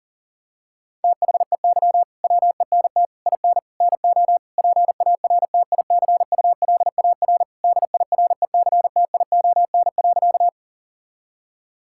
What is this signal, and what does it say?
Morse code, heard as a keyed tone, about 40 wpm, 700 Hz. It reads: THEY WENT IN NO PARTICULAR DIRECTION.
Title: Sentences from Top 1000 Words - Review Part 5 - 40wpm